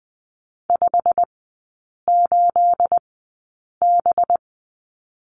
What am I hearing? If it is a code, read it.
Morse code, heard as a keyed tone, about 20 wpm, spaced as for 8 wpm, 690 Hz. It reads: 58B
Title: 212 Lesson 34e - Number 8 - Three Characters 20wpm